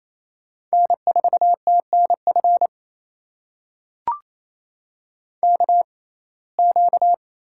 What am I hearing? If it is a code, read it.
Morse code, heard as a keyed tone, about 28 wpm, spaced as for 20 wpm, 700 Hz. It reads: N4TNF KQ
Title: USA Call Signs - 5 Characters - 6x Character Spacing - 28wpm